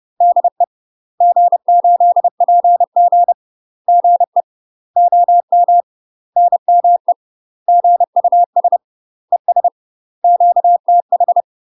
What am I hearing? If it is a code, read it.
Morse code, heard as a keyed tone, about 30 wpm, 700 Hz. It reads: DE G8PG GE OM NME GUS ES QTH
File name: UK MASTER QSOs TRAINING / 30WPM / UK MASTER QSOs 0000